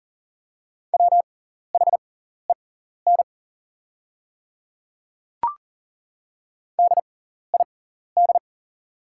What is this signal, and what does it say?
Morse code, heard as a keyed tone, about 40 wpm, spaced as for 14 wpm, 700 Hz. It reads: WHEN DID
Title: 6x Character Spacing - Top 100 Words - 40wpm